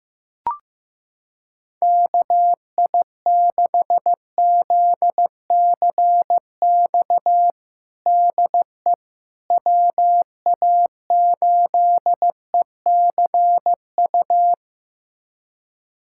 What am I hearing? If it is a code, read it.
Morse code, heard as a keyed tone, about 15 wpm, 700 Hz. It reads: KI6ZCX DE WA8ECU
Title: USA Callsign DE Callsign - 6 Characters - 15wpm